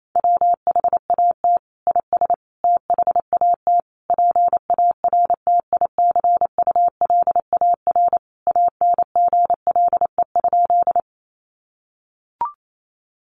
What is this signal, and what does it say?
Morse code, heard as a keyed tone, about 28 wpm, 700 Hz. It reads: WHAT IS THAT PARTICULAR ANGLE?
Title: Sentences from Top 1500 Words 28wpm